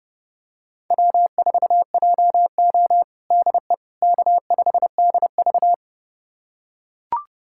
Morse code, heard as a keyed tone, about 32 wpm, 700 Hz. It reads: W4JO DE K5DV
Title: USA Callsign DE Callsign - 4 Characters - 30wpm